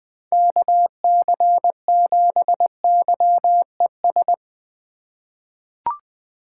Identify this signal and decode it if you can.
Morse code, heard as a keyed tone, about 21 wpm, 700 Hz. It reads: KC7YES
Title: USA Call Signs - 6 Characters - 3x Character Spacing - 20wpm